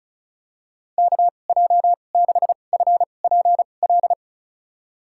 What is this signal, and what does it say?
Morse code, heard as a keyed tone, about 34 wpm, spaced as for 25 wpm, 700 Hz. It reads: KJ6FPL
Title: USA Call Signs - 6 Characters - 2x Character Spacing - 35wpm